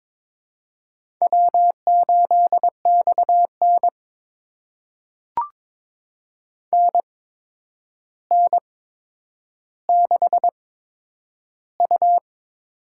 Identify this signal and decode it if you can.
Morse code, heard as a keyed tone, about 22 wpm, spaced as for 9 wpm, 700 Hz. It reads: W8XN NN6U
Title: USA Call Signs - 4 Characters - 8x Character Spacing - 22wpm